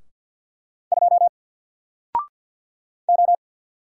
Morse code, heard as a keyed tone, about 48 wpm, 700 Hz. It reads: 2 O
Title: Single Character (Letter:Number:Punctuation:Prosign) Alt 50wpm